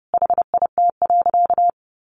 Morse code, heard as a keyed tone, about 30 wpm, 700 Hz. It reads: HIT.
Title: Sentences from Top 1200 Words 30wpm